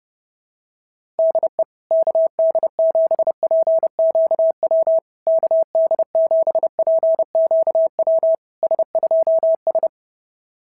Morse code, heard as a keyed tone, about 30 wpm, 650 Hz. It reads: DE KD7PQW KD7PQW S2S
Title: SOTA QSO Practice 30wpm